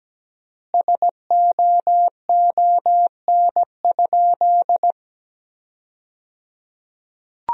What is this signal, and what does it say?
Morse code, heard as a keyed tone, about 17 wpm, 700 Hz. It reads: SOON?
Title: 3x Word-Spacing - Sentences from Top 400 Words - 17wpm